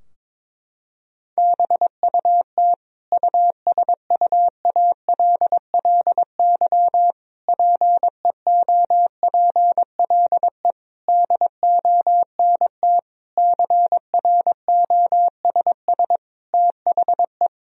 Morse code, heard as a keyed tone, about 22 wpm, 700 Hz. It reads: BUT USUALLY PEOPLE DONT CROSS THE
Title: Sentences from Top 1000 Words - Review Part 3 - 22wpm